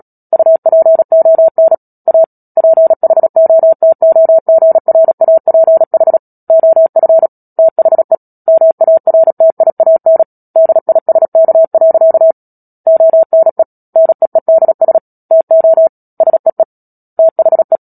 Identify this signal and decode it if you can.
Morse code, heard as a keyed tone, about 37 wpm, 650 Hz. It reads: UPON A PHOTOGRAPH OF THE MARTIAN DISK. ONE NEEDS TO SEE THE
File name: Book Two / 10-Chapter 10 - Epilogue - 36 WPM